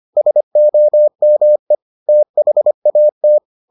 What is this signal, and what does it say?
Morse code, heard as a keyed tone, about 25 wpm, 600 Hz. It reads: SOME THAT